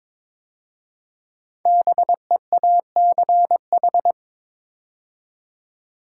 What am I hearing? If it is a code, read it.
Morse code, heard as a keyed tone, about 22 wpm, 700 Hz. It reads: BEACH
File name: 5 Letter Words - First 3 Letters Given - 22wpm